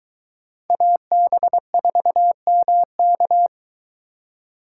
Morse code, heard as a keyed tone, about 22 wpm, 700 Hz. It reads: AB4MK